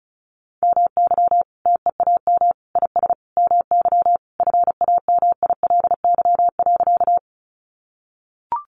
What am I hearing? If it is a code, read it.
Morse code, heard as a keyed tone, about 34 wpm, 700 Hz. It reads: MY TEAM IS MY FAMILY.
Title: Sentences from Top 1500 Words 35wpm